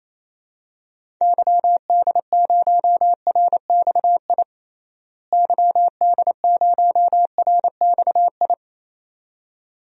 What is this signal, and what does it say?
Morse code, heard as a keyed tone, about 28 wpm, 700 Hz. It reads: YD0RXI YD0RXI
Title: DXCC-100 Call Signs - 6 Characters - Repeated 2x - 28wpm